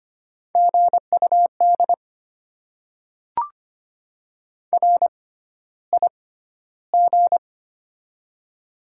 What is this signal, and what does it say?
Morse code, heard as a keyed tone, about 25 wpm, spaced as for 13 wpm, 690 Hz. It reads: GUD RIG
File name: Top 20 QSO Elements - 6x Character Spacing - 25wpm